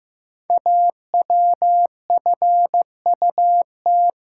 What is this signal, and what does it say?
Morse code, heard as a keyed tone, about 15 wpm, 700 Hz. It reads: AWFUT
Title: Code Groups - 5 Letters - 15wpm